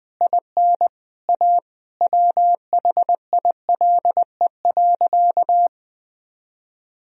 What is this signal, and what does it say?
Morse code, heard as a keyed tone, about 20 wpm, 700 Hz. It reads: IN A WHILE.
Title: Sentences from Top 500 Words and Abbreviations 20wpm